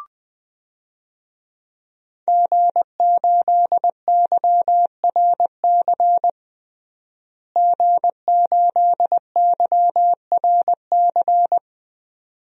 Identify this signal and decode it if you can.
Morse code, heard as a keyed tone, about 21 wpm, 700 Hz. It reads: G8YRC G8YRC
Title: DXCC-100 Call Signs - 5 Characters - Repeated 2x - 20wpm